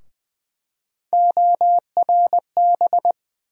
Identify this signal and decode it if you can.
Morse code, heard as a keyed tone, about 20 wpm, 700 Hz. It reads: ORB
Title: USA Call Sign Postfixes - 3 Letters 20wpm